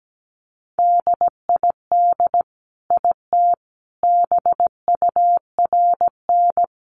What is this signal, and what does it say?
Morse code, heard as a keyed tone, about 17 wpm, 700 Hz. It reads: DID IT BURN
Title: Sentences from Top 1300 Words 17wpm